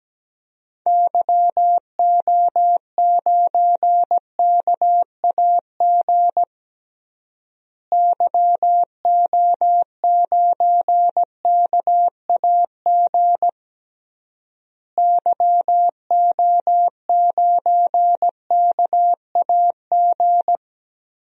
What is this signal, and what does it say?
Morse code, heard as a keyed tone, about 17 wpm, 700 Hz. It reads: YO9KAG YO9KAG YO9KAG
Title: DXCC-100 Call Signs - 6 Characters - Repeated 3x - 17wpm